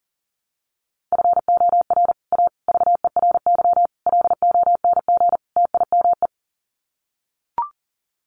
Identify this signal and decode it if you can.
Morse code, heard as a keyed tone, about 40 wpm, 700 Hz. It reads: FOR A VERY LONG TIME
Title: Sets of 5 Words 40wpm